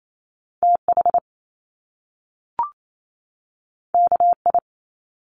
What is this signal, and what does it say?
Morse code, heard as a keyed tone, about 28 wpm, 700 Hz. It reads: TH KI
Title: Top Two Letter Combinations in Words 28wpm